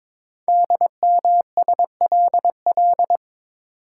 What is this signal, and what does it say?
Morse code, heard as a keyed tone, about 22 wpm, 700 Hz. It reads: DMSLL
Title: Code Groups - 5 Letter:Numbers - 22wpm